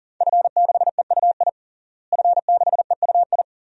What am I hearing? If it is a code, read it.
Morse code, heard as a keyed tone, about 40 wpm, 700 Hz. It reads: F6EUI F6EUI
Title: DXCC-100 Call Signs - 5 Characters - Repeated 2x - 40wpm